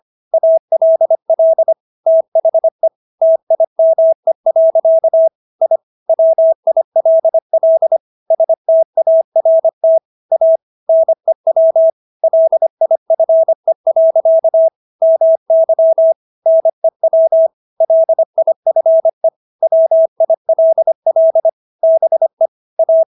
Morse code, heard as a keyed tone, about 25 wpm, 650 Hz. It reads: ALL THE TIME. I WILL START A NEW LIFE. MY NEW LIFE WILL BE A